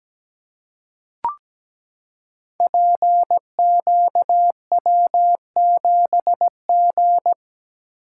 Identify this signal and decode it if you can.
Morse code, heard as a keyed tone, about 17 wpm, 700 Hz. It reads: PQW7G
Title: Code Groups - 5 Letter:Numbers - 17wpm